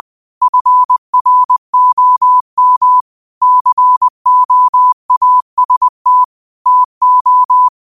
Morse code, heard as a keyed tone, about 20 wpm, 1000 Hz. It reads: FROM COAST TO
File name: qotdfiles / 20wpm